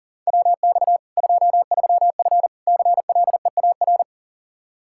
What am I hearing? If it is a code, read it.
Morse code, heard as a keyed tone, about 37 wpm, 700 Hz. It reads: WX 23F CLEAR
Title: K1USN SST QSO Practice 40wpm